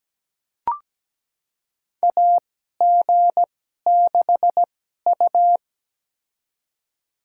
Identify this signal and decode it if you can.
Morse code, heard as a keyed tone, about 17 wpm, spaced as for 12 wpm, 700 Hz. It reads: AG6U
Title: USA Call Signs - 4 Characters - 2x Character Spacing - 17wpm